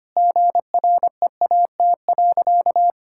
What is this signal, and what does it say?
Morse code, heard as a keyed tone, about 25 wpm, 700 Hz. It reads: GREAT.
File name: practice / 25wpm / Sentences with 4-Word Phrases - 25wpm